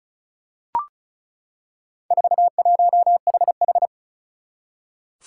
Morse code, heard as a keyed tone, about 34 wpm, 700 Hz. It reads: 41HH
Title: Four Challenging Letter-Number Combinations 35wpm